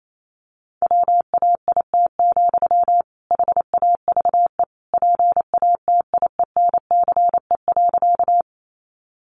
Morse code, heard as a keyed tone, about 28 wpm, 700 Hz. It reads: WAIT, HAVE PATIENCE.
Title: Sentences from Top 1300 Words 28wpm